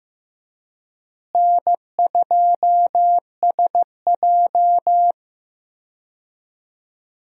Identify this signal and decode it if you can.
Morse code, heard as a keyed tone, about 15 wpm, 700 Hz. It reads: N2SJ
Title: USA Call Signs - 4 Characters - Repeated 2x - 15wpm